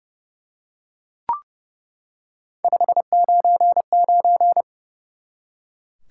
Morse code, heard as a keyed tone, about 30 wpm, 700 Hz. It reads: 599
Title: Top 90 QSO Elements 30wpm